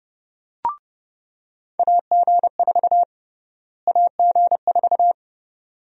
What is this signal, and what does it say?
Morse code, heard as a keyed tone, about 32 wpm, 700 Hz. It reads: AG4 AG4
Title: USA Call Sign Prefixes - Repeated 2x - 30wpm